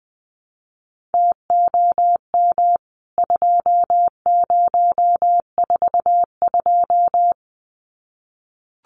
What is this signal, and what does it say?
Morse code, heard as a keyed tone, about 19 wpm, 700 Hz. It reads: TOM 2042
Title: CWT Exchange 20wpm